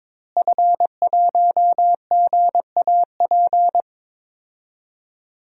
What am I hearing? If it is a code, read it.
Morse code, heard as a keyed tone, about 22 wpm, 700 Hz. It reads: F1GAP
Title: DXCC-100 Call Signs - 5 Characters - Repeated 3x - 22wpm